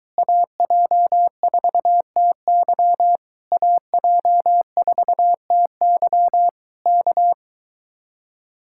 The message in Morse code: AJ4TY AJ4TY K

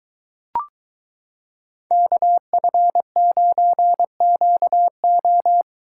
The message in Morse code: KF9QO

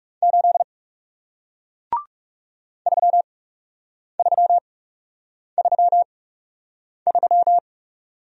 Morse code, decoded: 8 3333

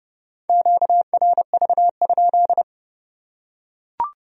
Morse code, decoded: QRV?

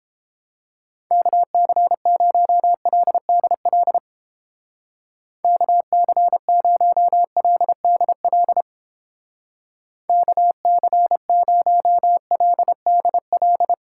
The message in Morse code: KC0LDL KC0LDL KC0LDL

